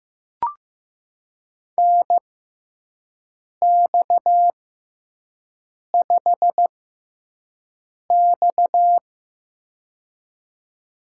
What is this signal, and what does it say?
Morse code, heard as a keyed tone, about 15 wpm, spaced as for 5 wpm, 700 Hz. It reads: NX5X
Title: USA Call Signs - 4 Characters - 6x Character Spacing - 15wpm